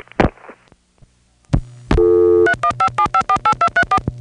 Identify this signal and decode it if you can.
DTMF, heard as telephone touch tones, tones of about 80 ms. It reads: 3157548234